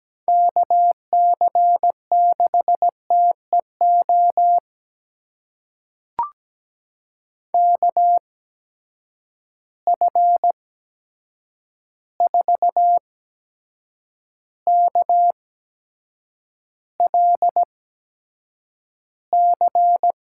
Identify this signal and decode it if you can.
Morse code, heard as a keyed tone, about 17 wpm, spaced as for 7 wpm, 700 Hz. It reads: KC6TEO KF4KLC
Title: USA Call Signs - 6 Characters - 8x Character Spacing - 17wpm